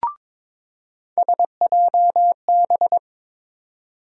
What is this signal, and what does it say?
Morse code, heard as a keyed tone, about 22 wpm, 700 Hz. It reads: SJB
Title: USA Call Sign Postfixes - 3 Letters 22wpm